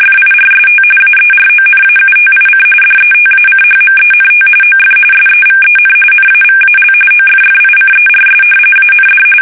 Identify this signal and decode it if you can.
RTTY, about 45.45 baud. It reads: FIRST AERONAUTIC DETACHMENT OF SEVEN NAVAL OFFICERS AND
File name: fsk850